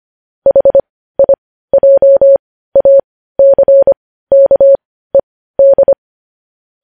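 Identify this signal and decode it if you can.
Morse code, heard as a keyed tone, about 25 wpm, spaced as for 15 wpm, 550 Hz. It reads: HIJACKED
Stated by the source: Morse code news headlines sent at 15 wpm.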